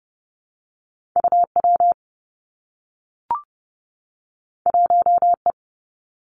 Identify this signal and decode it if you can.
Morse code, heard as a keyed tone, about 30 wpm, 700 Hz. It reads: UW 1E
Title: Two Challenging Letter-Number Combinations 30wpm